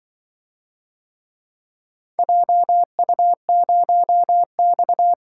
Morse code, heard as a keyed tone, about 25 wpm, 700 Hz. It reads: JU0X
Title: Speed-Racing - Four Letter-Number Combinations - 20wpm